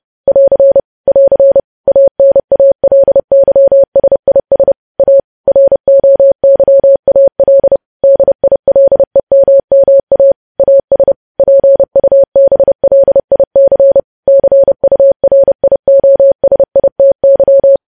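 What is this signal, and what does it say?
Morse code, heard as a keyed tone, about 30 wpm, 550 Hz. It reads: <AR> <AR> ANALYSIS A ROYAL DILEMMA AS PUBLIC CURIOSITY
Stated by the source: Morse code sending of news headlines at 30 wpm.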